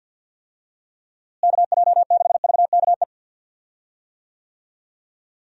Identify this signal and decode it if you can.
Morse code, heard as a keyed tone, about 48 wpm, 700 Hz. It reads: KJ6VKE
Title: USA Call Signs - 6 Characters - Repeated 3x - 50wpm